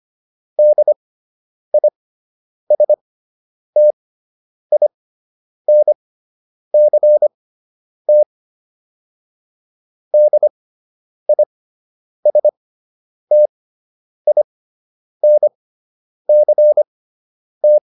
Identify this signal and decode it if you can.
Morse code, heard as a keyed tone, about 25 wpm, spaced as for 9 wpm, 600 Hz. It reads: DISTINCT DISTINCT